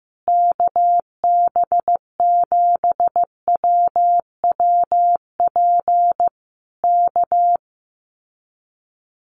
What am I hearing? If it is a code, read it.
Morse code, heard as a keyed tone, about 15 wpm, 700 Hz. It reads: KB7WWP K
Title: Common QSO Phrases Encore 15wpm